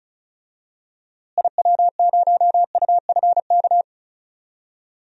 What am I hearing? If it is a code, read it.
Morse code, heard as a keyed tone, about 36 wpm, 700 Hz. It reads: IW0UFK
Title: DXCC-100 Call Signs - 6 Characters - Repeated 2x - 35wpm